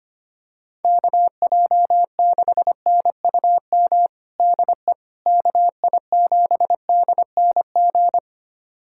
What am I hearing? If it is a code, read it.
Morse code, heard as a keyed tone, about 25 wpm, 700 Hz. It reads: KJ6NUM DE KI7DNG
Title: USA Callsign DE Callsign - 6 Characters - 25wpm